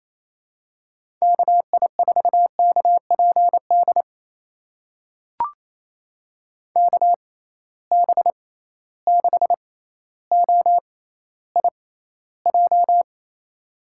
Morse code, heard as a keyed tone, about 27 wpm, spaced as for 14 wpm, 700 Hz. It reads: KI4KPD KB6OIJ
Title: USA Call Signs - 6x Character Spacing - 28wpm